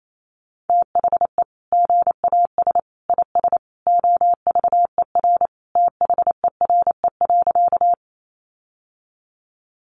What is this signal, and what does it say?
Morse code, heard as a keyed tone, about 28 wpm, 700 Hz. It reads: THE GAS IS OVER THERE.
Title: Sentences from Top 1100 Words 28wpm